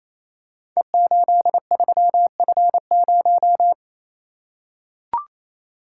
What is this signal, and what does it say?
Morse code, heard as a keyed tone, about 29 wpm, 700 Hz. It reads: E83F0
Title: Code Groups - 5 Letter:Numbers - 28wpm